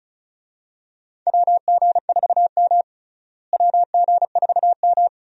Morse code, heard as a keyed tone, about 34 wpm, 700 Hz. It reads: WG4M WG4M
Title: USA Call Signs - 4 Characters - Repeated 2x - 35wpm